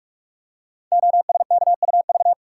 Morse code, heard as a keyed tone, about 44 wpm, 700 Hz. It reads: OSKUV
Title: Code Groups - 5 Letters - 45wpm